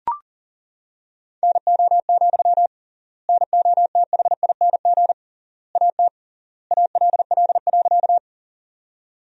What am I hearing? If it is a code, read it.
Morse code, heard as a keyed tone, about 37 wpm, 700 Hz. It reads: NO, NOTHING AT ALL.
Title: 3x Word-Spacing - Sentences from Top 500 Words - 40wpm